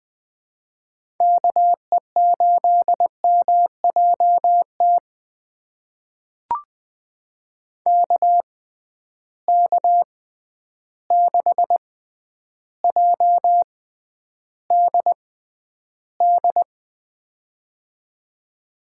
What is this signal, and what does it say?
Morse code, heard as a keyed tone, about 21 wpm, spaced as for 10 wpm, 700 Hz. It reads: KE8MJT KK6JDD
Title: USA Call Signs - 6 Characters - 6x Character Spacing - 20wpm